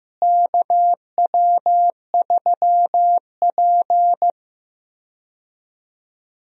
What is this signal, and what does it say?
Morse code, heard as a keyed tone, about 15 wpm, 700 Hz. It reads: KW3P
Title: USA Call Signs - 4 Characters - Repeated 3x - 15wpm